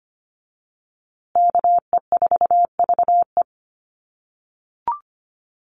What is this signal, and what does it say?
Morse code, heard as a keyed tone, about 25 wpm, 700 Hz. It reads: KE4VE
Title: USA Call Signs Repeated 3x 25wpm